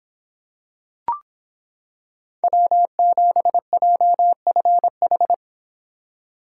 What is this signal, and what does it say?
Morse code, heard as a keyed tone, about 25 wpm, 700 Hz. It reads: W7JFH